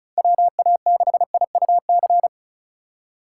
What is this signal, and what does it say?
Morse code, heard as a keyed tone, about 36 wpm, 700 Hz. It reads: WA6IUC
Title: USA Call Signs - 6 Characters - 4x Character Spacing - 35wpm